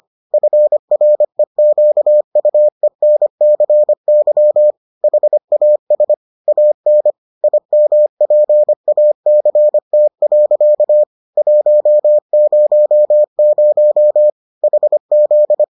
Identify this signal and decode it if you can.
Morse code, heard as a keyed tone, about 25 wpm, 600 Hz. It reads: FREQUENCY HAS AN IMPACT. 100 HZ